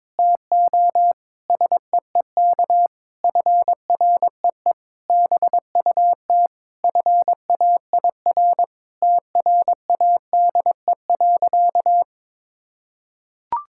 Morse code, heard as a keyed tone, about 22 wpm, 700 Hz. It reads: TO SEEK FREE BUT FAIR TRADE.
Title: Sentences from Top 1600 Words 22wpm